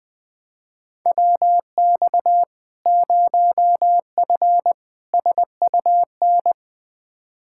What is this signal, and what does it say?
Morse code, heard as a keyed tone, about 20 wpm, 700 Hz. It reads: WX 0F SUN
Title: Practice QSO - Exchange 2 - Info - 20wpm